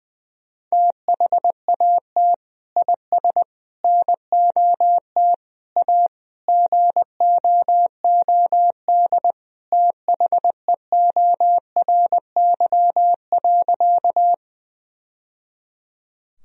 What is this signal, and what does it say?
Morse code, heard as a keyed tone, about 20 wpm, 700 Hz. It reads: THAT IS NOT A GOOD THEORY.
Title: Sentences from Top 1000 Words - Review Part 3 - 20wpm